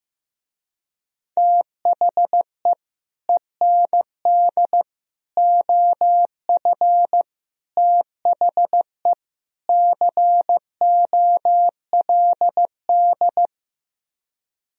Morse code, heard as a keyed tone, about 15 wpm, 700 Hz. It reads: THE END OF THE COLD
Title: Speed-Racing - Sets of 5 Words 15wpm